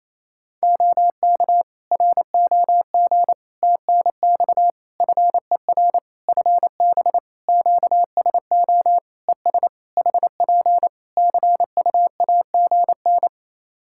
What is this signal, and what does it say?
Morse code, heard as a keyed tone, about 28 wpm, 700 Hz. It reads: OK ROG TNX FER FB QSO ES HP CUAGN
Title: Practice QSO - Exchange 4 - Ending - 28wpm